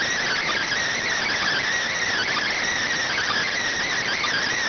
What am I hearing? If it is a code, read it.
Emergency Alert System (SAME) tones (header bursts), heard as harsh buzzing bursts. It is heard from 0.2 to 4.5 s.